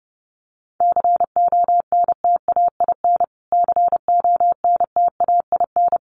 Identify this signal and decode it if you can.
Morse code, heard as a keyed tone, about 30 wpm, 700 Hz. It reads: CONTAIN CONTAIN
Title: Top 500 Words Repeated 30wpm